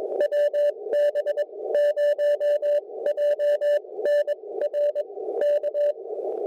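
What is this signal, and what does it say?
Morse code, heard as a keyed tone, about 18 wpm, 600 Hz. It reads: WB0JNR K
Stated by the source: QRP SOTA contact between N5YCO and WB0JNR on Berrian Mountain, 20 meters.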